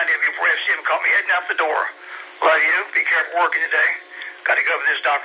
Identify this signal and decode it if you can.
Emergency Alert System (SAME) tones, heard as harsh buzzing bursts, with speech. It is heard from 1.0 to 5.0 s.